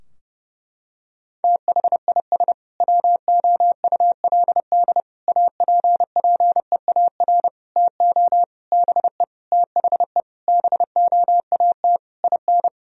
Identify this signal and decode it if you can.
Morse code, heard as a keyed tone, about 31 wpm, 700 Hz. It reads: THIS WOULD APPEAR TO BE THE BOAT IN
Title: Sentences from Top 1000 Words - Review Part 3 - 30wpm